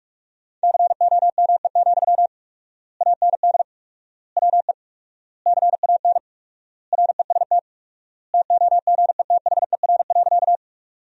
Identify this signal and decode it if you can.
Morse code, heard as a keyed tone, about 47 wpm, 700 Hz. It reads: COME, AND WE CAN REST TOGETHER.
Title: 4x Word-Spacing - Sentences from Top 500 Words - 45wpm